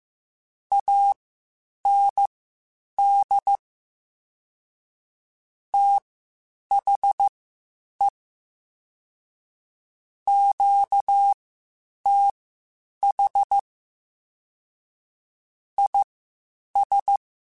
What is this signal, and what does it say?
Morse code, heard as a keyed tone, about 15 wpm, spaced as for 8 wpm, 800 Hz. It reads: AND THE QTH IS
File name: (spaced) / QSOs Spaced-mp3s / kc7haa @13 sp